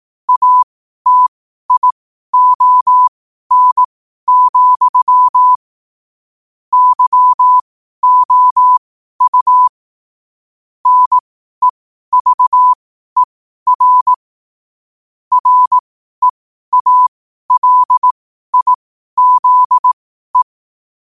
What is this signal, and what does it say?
Morse code, heard as a keyed tone, about 18 wpm, spaced as for 12 wpm, 1000 Hz. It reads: ATION, YOU NEVER REALIZE
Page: 12 wpm morse code quotes